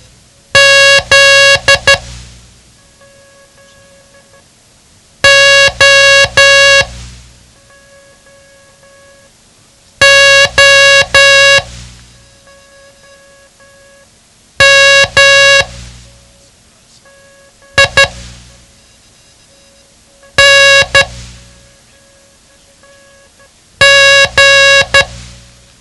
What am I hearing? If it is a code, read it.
Morse code, heard as a keyed tone, about 12 wpm, spaced as for 3 wpm, 1650 Hz. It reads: ZOOMING
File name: ham radio / CodeQuick / Code Quick 07